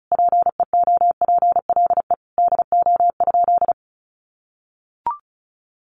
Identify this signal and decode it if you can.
Morse code, heard as a keyed tone, about 34 wpm, 700 Hz. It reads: PEOPLE DO?
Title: Sentences from Top 1100 Words 35wpm